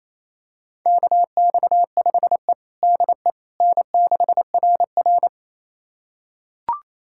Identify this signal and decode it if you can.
Morse code, heard as a keyed tone, about 29 wpm, 700 Hz. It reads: KX5E DE N6RR
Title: USA Callsign DE Callsign - 4 Characters - 28wpm